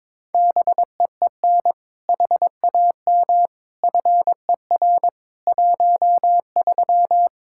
Morse code, heard as a keyed tone, about 23 wpm, 700 Hz. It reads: BEEN HAM FER 13
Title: Practice QSO - Exchange 3 - More Info - 22wpm